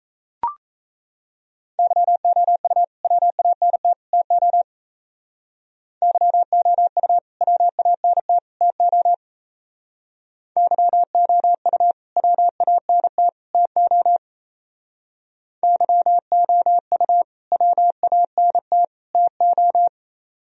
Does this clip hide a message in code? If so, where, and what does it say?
Yes: Morse code from 1.8 to 19.9 s, heard as a keyed tone, about 35 wpm, 700 Hz. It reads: YOU WANT TO YOU WANT TO YOU WANT TO YOU WANT TO